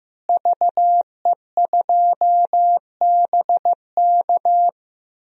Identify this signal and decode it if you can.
Morse code, heard as a keyed tone, about 15 wpm, 700 Hz. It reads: VE2BK
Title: DXCC-100 Call Signs - 5 Characters - Repeated 3x - 15wpm